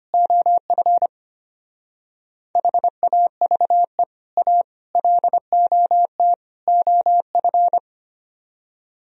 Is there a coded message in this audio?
Yes: Morse code from 0.1 to 7.8 s, heard as a keyed tone, about 31 wpm, 700 Hz. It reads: OF HAVE A LOT OF